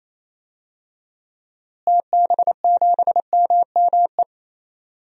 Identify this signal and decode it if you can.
Morse code, heard as a keyed tone, about 28 wpm, 700 Hz. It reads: TB7MME
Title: DXCC-100 Call Signs - 6 Characters - Repeated 3x - 28wpm